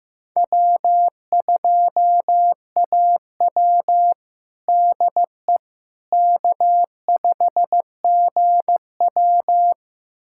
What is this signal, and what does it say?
Morse code, heard as a keyed tone, about 15 wpm, 700 Hz. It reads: W2AW DE K5GW
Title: USA Callsign DE Callsign - 4 Characters - 15wpm